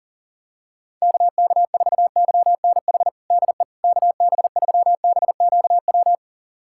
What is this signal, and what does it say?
Morse code, heard as a keyed tone, about 40 wpm, 700 Hz. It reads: KK4YNH DE KB3BQW
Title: USA Callsign DE Callsign - 6 Characters - 40wpm